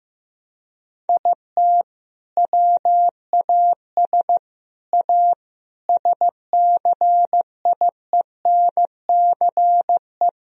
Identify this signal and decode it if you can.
Morse code, heard as a keyed tone, about 15 wpm, 700 Hz. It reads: IT WAS A SCIENCE
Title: Sentences from Top 1000 Words - Review Part 2 - 15wpm